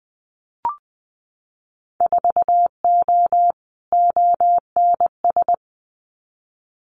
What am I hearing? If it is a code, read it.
Morse code, heard as a keyed tone, about 20 wpm, 700 Hz. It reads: 4O ONS
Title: Winter Field Day Exchanges 20wpm